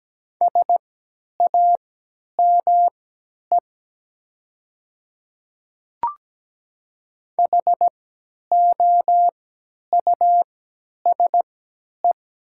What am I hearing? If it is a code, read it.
Morse code, heard as a keyed tone, about 17 wpm, spaced as for 10 wpm, 700 Hz. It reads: SAME HOUSE
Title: 3x Character Spacing - Top 200 Words - 17wpm